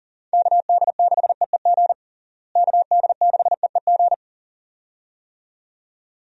Morse code, heard as a keyed tone, about 37 wpm, 700 Hz. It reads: KD6EEG KD6EEG
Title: USA Call Signs - 6 Characters - Repeated 2x - 40wpm